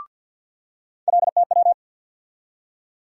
Morse code, heard as a keyed tone, about 51 wpm, 700 Hz. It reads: LTW
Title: USA Call Sign Postfixes - 3 Letters 50wpm